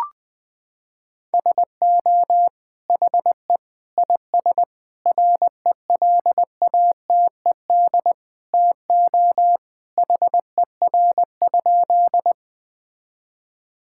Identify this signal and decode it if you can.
Morse code, heard as a keyed tone, about 21 wpm, 700 Hz. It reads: SO HE IS RELATED TO HER?
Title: Sentences from Top 1000 Words - Review Part 5 - 20wpm